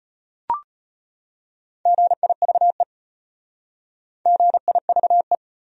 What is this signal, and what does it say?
Morse code, heard as a keyed tone, about 32 wpm, 700 Hz. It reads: GIVE GIVE